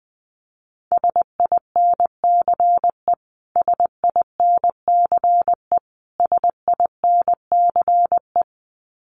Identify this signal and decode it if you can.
Morse code, heard as a keyed tone, about 20 wpm, 700 Hz. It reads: SINCE SINCE SINCE
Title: Top 500 Words Repeated 20wpm